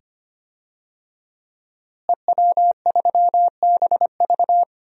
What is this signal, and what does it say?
Morse code, heard as a keyed tone, about 25 wpm, 700 Hz. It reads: EW3BV
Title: DXCC-100 Call Signs - 5 Characters - Repeated 2x - 25wpm